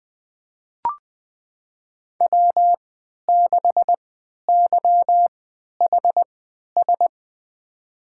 Morse code, heard as a keyed tone, about 20 wpm, spaced as for 11 wpm, 700 Hz. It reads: W6YHS
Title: USA Call Signs - 5 Characters - 3x Character Spacing - 20wpm